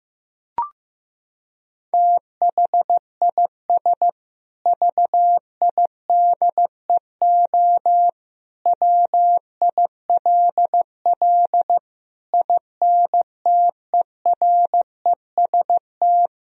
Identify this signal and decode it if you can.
Morse code, heard as a keyed tone, about 15 wpm, 700 Hz. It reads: THIS VIDEO WILL INTEREST
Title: Sentences from Top 1000 Words - Review Part 4 - 15wpm